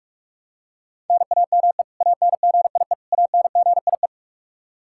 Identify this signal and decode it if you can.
Morse code, heard as a keyed tone, about 45 wpm, 700 Hz. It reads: NAME ANGIE ANGIE
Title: K1USN SST QSO Practice 45wpm